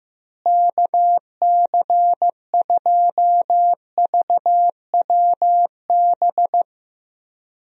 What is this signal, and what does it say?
Morse code, heard as a keyed tone, about 15 wpm, 700 Hz. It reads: KC2VWB
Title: USA Call Signs - 6 Characters - Repeated 3x - 15wpm